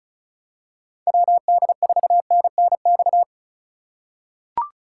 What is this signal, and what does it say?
Morse code, heard as a keyed tone, about 36 wpm, 700 Hz. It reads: WD4NNX
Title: USA Call Signs - 6 Characters - 3x Character Spacing - 35wpm